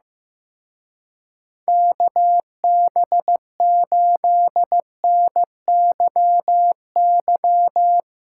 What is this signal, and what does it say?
Morse code, heard as a keyed tone, about 17 wpm, 700 Hz. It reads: KB8NYY
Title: Speed Racing - USA 2x3 Call Signs 15wpm